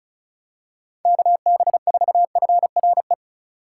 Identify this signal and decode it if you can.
Morse code, heard as a keyed tone, about 35 wpm, 690 Hz. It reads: KB4FRE
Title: USA Call Signs - 6 Characters - 6x Character Spacing - 35wpm